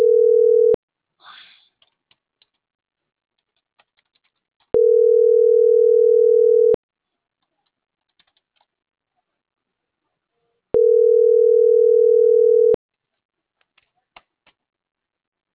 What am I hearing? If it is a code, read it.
Ringback tone (North America), 440 + 480 Hz.